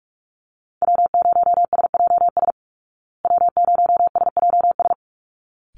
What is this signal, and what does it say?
Morse code, heard as a keyed tone, about 44 wpm, 700 Hz. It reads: W0SJS W0SJS
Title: Call Signs Repeated 3x - 45wpm